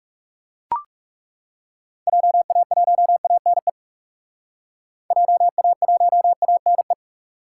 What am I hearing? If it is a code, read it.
Morse code, heard as a keyed tone, about 37 wpm, 700 Hz. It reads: JA1ANE JA1ANE
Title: Speed Racing - USA 2x3 Call Signs 30wpm